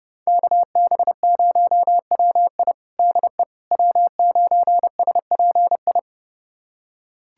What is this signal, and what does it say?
Morse code, heard as a keyed tone, about 30 wpm, 700 Hz. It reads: KB0WI DE W9SPI
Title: USA Callsign DE Callsign - 5 Characters - 30wpm